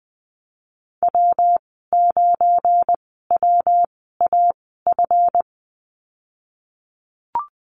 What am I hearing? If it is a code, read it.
Morse code, heard as a keyed tone, about 20 wpm, spaced as for 14 wpm, 700 Hz. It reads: W9WAF
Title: 186 Lesson 30g - Letter b - Call Signs 20wpm